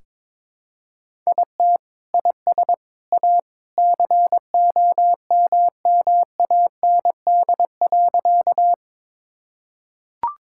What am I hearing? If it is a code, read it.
Morse code, heard as a keyed tone, about 22 wpm, 700 Hz. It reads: IT IS A COMMAND.
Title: Sentences from Top 1600 Words 22wpm